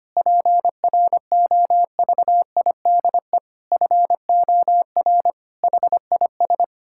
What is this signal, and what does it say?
Morse code, heard as a keyed tone, about 25 wpm, 700 Hz. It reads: PROVIDE FOR HIS
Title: Sentences from Top 1000 Words - Review Part 5 - 25wpm